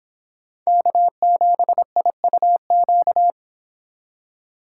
Morse code, heard as a keyed tone, about 26 wpm, 700 Hz. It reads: K7IUQ